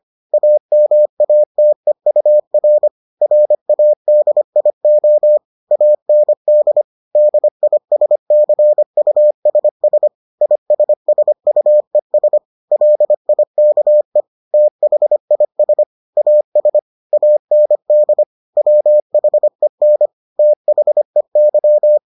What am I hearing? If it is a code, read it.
Morse code, heard as a keyed tone, about 25 wpm, 600 Hz. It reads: AMATEUR RADIO AND DISCUSS ISSUES LIKE THIS AS AND WHEN THEY